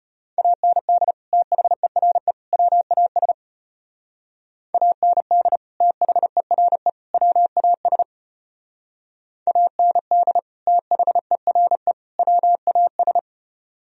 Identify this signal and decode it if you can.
Morse code, heard as a keyed tone, about 30 wpm, 700 Hz. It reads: AND THERE WAS AND THERE WAS AND THERE WAS
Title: Speed-Racing - Sets of 3 Words 25wpm